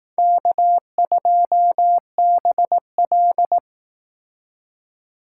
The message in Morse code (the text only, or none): K2BL